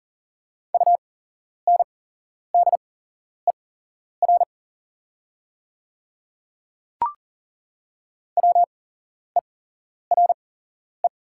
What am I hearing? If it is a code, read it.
Morse code, heard as a keyed tone, about 40 wpm, spaced as for 11 wpm, 690 Hz. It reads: UNDER WERE
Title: 8x Character Spacing - Top 500 Words - 40wpm